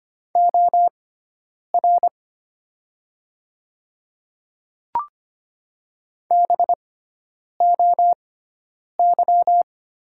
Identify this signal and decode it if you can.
Morse code, heard as a keyed tone, about 25 wpm, spaced as for 9 wpm, 690 Hz. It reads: OR BOY
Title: 6x Character Spacing - Top 200 Words - 25wpm